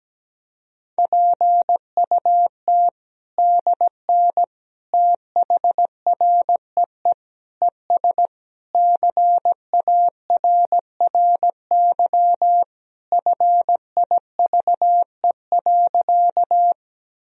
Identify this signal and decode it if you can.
Morse code, heard as a keyed tone, about 17 wpm, 700 Hz. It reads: PUT DN THREE ES CARRY FIVE.
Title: Sentences from Top 400 Words and Abbreviations 17wpm